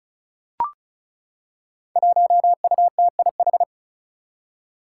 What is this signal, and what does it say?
Morse code, heard as a keyed tone, about 36 wpm, 700 Hz. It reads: 1UTIH
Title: Five Challenging Letter-Number Combinations 35wpm